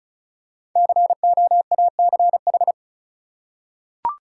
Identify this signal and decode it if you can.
Morse code, heard as a keyed tone, about 34 wpm, 700 Hz. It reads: COACH
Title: Five Letter Words - 35wpm